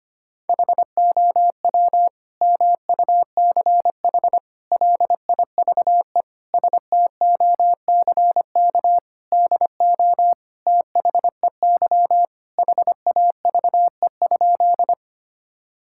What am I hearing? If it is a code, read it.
Morse code, heard as a keyed tone, about 25 wpm, 700 Hz. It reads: HOW MUCH LIVE STOCK DO THEY HAVE?
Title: Sentences from Top 1000 Words - Review Part 5 - 25wpm